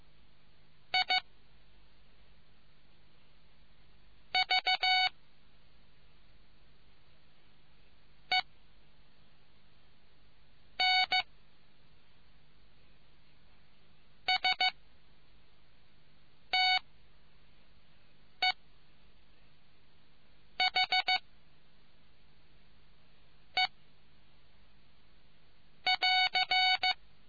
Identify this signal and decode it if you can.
Morse code, heard as a keyed tone, about 15 wpm, 750 Hz. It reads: I V E N S T E H E <AR>